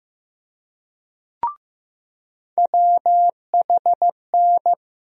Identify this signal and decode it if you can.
Morse code, heard as a keyed tone, about 15 wpm, 700 Hz. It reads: WHN
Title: USA Call Sign Postfixes - 3 Letters 15wpm